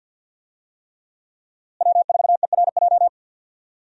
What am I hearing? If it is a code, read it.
Morse code, heard as a keyed tone, about 52 wpm, 700 Hz. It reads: W4ERJ
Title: USA Call Signs - 5 Characters - Repeated 3x - 50wpm